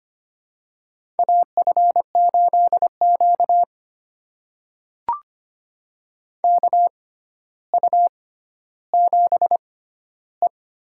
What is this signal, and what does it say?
Morse code, heard as a keyed tone, about 25 wpm, spaced as for 13 wpm, 700 Hz. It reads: AF8Q KU7E
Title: USA Call Signs - 4 Characters - 6x Character Spacing - 25wpm